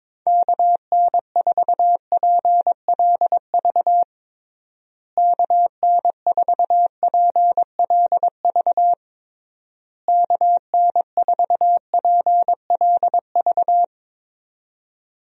Morse code, heard as a keyed tone, about 22 wpm, 700 Hz. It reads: KN4PLV KN4PLV KN4PLV
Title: USA Call Signs - 6 Characters - Repeated 3x - 22wpm